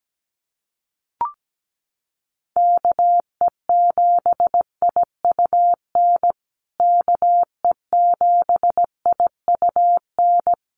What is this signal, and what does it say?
Morse code, heard as a keyed tone, about 17 wpm, 700 Hz. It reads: KE7IUN KE7IUN
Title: USA Call Signs Repeated 3x 17wpm